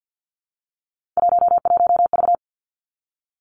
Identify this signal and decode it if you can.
Morse code, heard as a keyed tone, about 50 wpm, 700 Hz. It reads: 11V
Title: Three Challenging Letter-Number Combinations 50wpm